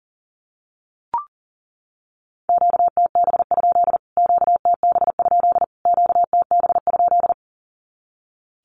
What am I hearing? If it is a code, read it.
Morse code, heard as a keyed tone, about 40 wpm, 700 Hz. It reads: QTB? QTB? QTB?
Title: Q-Codes Repeated 3x 40wpm